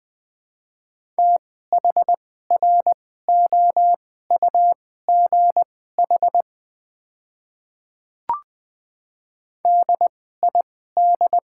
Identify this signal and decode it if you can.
Morse code, heard as a keyed tone, about 20 wpm, spaced as for 14 wpm, 700 Hz. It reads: THROUGH DID
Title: 2x Character Spacing - Top 200 Words - 20wpm